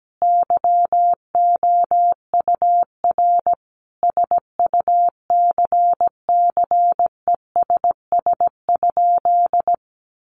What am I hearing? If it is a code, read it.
Morse code, heard as a keyed tone, about 17 wpm, 700 Hz. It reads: YOUR SUCCESS?
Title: Sentences from Top 900 Words 17wpm